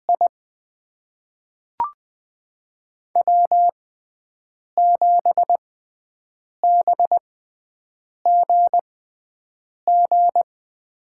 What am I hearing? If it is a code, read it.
Morse code, heard as a keyed tone, about 20 wpm, spaced as for 7 wpm, 700 Hz. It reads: I W7BGG
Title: USA Call Signs - 5 Characters - 6x Character Spacing - 20wpm